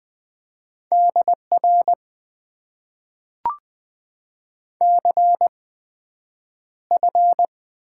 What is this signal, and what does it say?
Morse code, heard as a keyed tone, about 20 wpm, spaced as for 9 wpm, 700 Hz. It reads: DR CF
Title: Top 90 QSO Elements - 8x Character Spacing - 20wpm